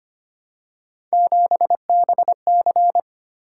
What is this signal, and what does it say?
Morse code, heard as a keyed tone, about 25 wpm, 700 Hz. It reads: 7BC
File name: Three Letter-Number Combinations - 25wpm